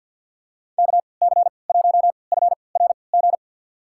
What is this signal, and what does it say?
Morse code, heard as a keyed tone, about 52 wpm, spaced as for 28 wpm, 700 Hz. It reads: KC1FRG
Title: USA Call Signs - 6 Characters - 3x Character Spacing - 50wpm